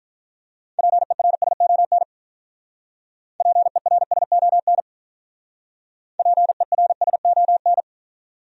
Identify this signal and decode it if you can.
Morse code, heard as a keyed tone, about 42 wpm, 700 Hz. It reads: PERSON PERSON PERSON
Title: Speed-Racing Top 500 Words 35wpm